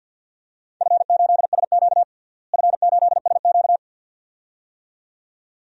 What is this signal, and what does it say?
Morse code, heard as a keyed tone, about 51 wpm, 700 Hz. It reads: F8SQ F8SQ
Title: DXCC-100 Call Signs - 4 Characters - Repeated 2x - 50wpm